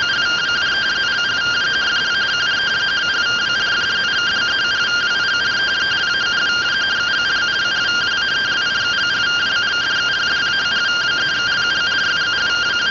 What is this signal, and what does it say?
Two-tone teleprinter signal, about 170 Hz shift.